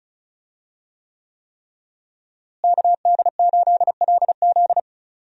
Morse code, heard as a keyed tone, about 36 wpm, 700 Hz. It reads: KD8LZ
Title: USA Call Signs - 5 Characters - Repeated 3x - 35wpm